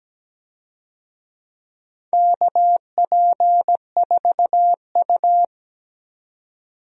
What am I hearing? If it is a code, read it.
Morse code, heard as a keyed tone, about 17 wpm, 700 Hz. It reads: KP4U
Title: DXCC-100 Call Signs - 4 Characters - 17wpm